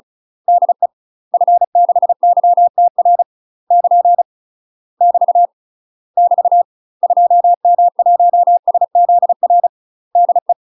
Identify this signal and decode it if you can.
Morse code, heard as a keyed tone, about 35 wpm, 700 Hz. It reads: DE F6YTR <KN> <BT> <BT> 2M1SZR DE
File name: UK MASTER QSOs TRAINING / 35WPM / UK MASTER QSOs 0039